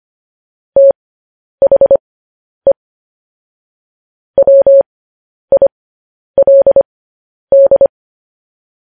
Morse code, heard as a keyed tone, about 25 wpm, spaced as for 10 wpm, 550 Hz. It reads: THE WILD
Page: news-headlines-in-morse-code-at-10-wpm